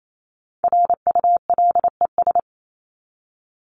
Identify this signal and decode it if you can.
Morse code, heard as a keyed tone, about 28 wpm, 700 Hz. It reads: RULES
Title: ES:S Suffix - Top 100 Words with Base Words 28wpm